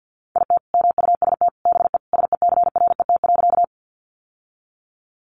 Morse code, heard as a keyed tone, about 50 wpm, 700 Hz. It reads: IT MUST BE SECRET.
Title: Sentences from Top 1200 Words 50wpm